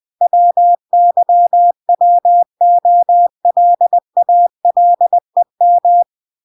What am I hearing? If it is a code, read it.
Morse code, heard as a keyed tone, about 20 wpm, 700 Hz. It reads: WYWOLALEM